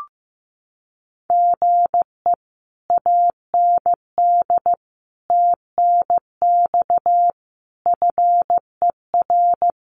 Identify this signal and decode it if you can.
Morse code, heard as a keyed tone, about 14 wpm, 700 Hz. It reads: GE AND TNX FER
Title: Common QSO Phrases Encore 15wpm